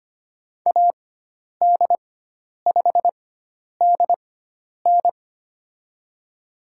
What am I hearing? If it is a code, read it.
Morse code, heard as a keyed tone, about 25 wpm, spaced as for 10 wpm, 700 Hz. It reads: AD5DN
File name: Lesson 16g - Letter c - Call Signs 25wpm